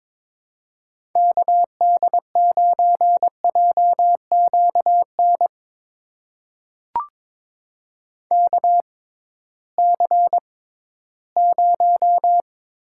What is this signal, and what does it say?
Morse code, heard as a keyed tone, about 22 wpm, spaced as for 14 wpm, 700 Hz. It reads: KD9JQN KC0
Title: USA Call Signs - 6 Characters - 6x Character Spacing - 22wpm